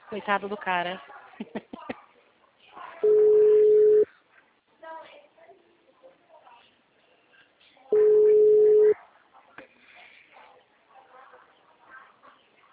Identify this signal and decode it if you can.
Ringback tone (Europe, 425 Hz), with speech.